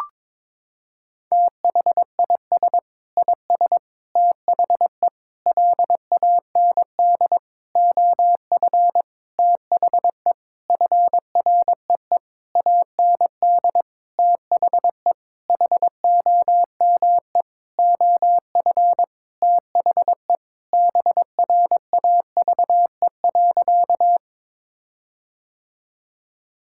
Morse code, heard as a keyed tone, about 22 wpm, 700 Hz. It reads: THIS IS THE LAND OF THE FREE AND THE HOME OF THE BRAVE.
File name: Sentences with 9-Word Phrases - 22wpm